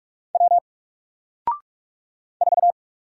Morse code, heard as a keyed tone, about 46 wpm, 700 Hz. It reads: W 4
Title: Single Letter-Number 45wpm